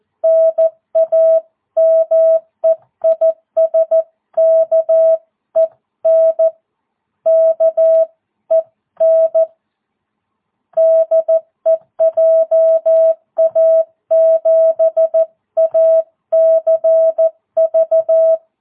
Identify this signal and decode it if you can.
Morse code, heard as a keyed tone, about 14 wpm, 650 Hz. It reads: NAME IS KEN KEN DE JA7ACV
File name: jh8cbh / 2022cw